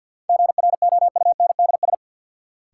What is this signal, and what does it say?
Morse code, heard as a keyed tone, about 53 wpm, 700 Hz. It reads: GROUNDS
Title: Sentences from Top 1300 Words - New Word Drill Alt - 50wpm